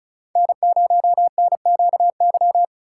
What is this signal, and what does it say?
Morse code, heard as a keyed tone, about 36 wpm, 700 Hz. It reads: N0NQY
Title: USA Call Signs - 4x Character Spacing - 35wpm